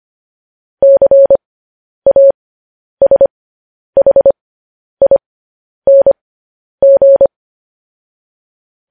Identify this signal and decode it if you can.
Morse code, heard as a keyed tone, about 25 wpm, spaced as for 10 wpm, 550 Hz. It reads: CASHING
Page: news-headlines-in-morse-code-at-10-wpm